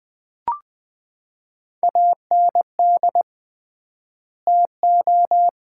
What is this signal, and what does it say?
Morse code, heard as a keyed tone, about 20 wpm, 700 Hz. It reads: AND TO
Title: 3x Word-Spacing - Sets of 2 Words - 20wpm